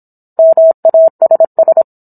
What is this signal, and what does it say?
Morse code, heard as a keyed tone, about 27 wpm, 650 Hz. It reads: MASS
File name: Book One / 17-Chapter 17 - 26 WPM